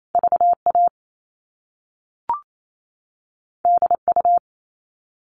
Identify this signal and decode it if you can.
Morse code, heard as a keyed tone, about 28 wpm, 700 Hz. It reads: VA DU
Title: All Cut Numbers - 2-Digit 28wpm